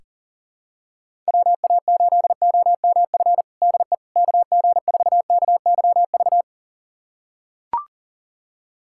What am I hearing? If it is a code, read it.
Morse code, heard as a keyed tone, about 40 wpm, 700 Hz. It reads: WA8OMF DE KG4KYV
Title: USA Callsign DE Callsign - 6 Characters - 40wpm